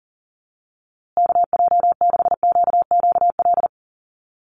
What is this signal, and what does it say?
Morse code, heard as a keyed tone, about 40 wpm, 700 Hz. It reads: KJ6QQL
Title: USA Call Signs Repeated 3x 40wpm